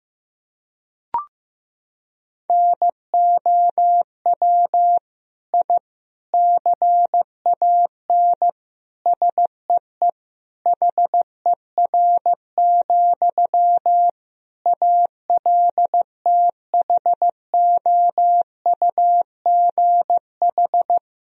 Morse code, heard as a keyed tone, about 15 wpm, 700 Hz. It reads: NOW I CAN SEE HER, ALTHOUGH
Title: Sentences from Top 1000 Words - Review Part 3 - 15wpm